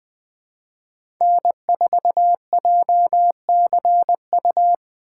Morse code, heard as a keyed tone, about 20 wpm, 700 Hz. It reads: N4JCU
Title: USA Call Signs - 5 Characters - 3x Character Spacing - 20wpm